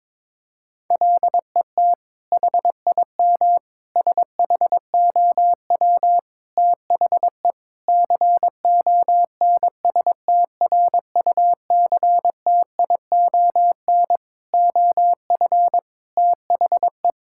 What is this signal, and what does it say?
Morse code, heard as a keyed tone, about 22 wpm, 700 Hz. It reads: LET HIM SHOW THE CONSTRUCTION OF THE
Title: Sentences from Top 1600 Words 22wpm